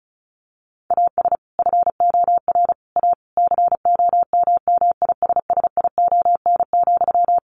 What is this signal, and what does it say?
Morse code, heard as a keyed tone, about 34 wpm, 700 Hz. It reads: AS FOR A COMMISSION,
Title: Sentences from Top 1400 Words 35wpm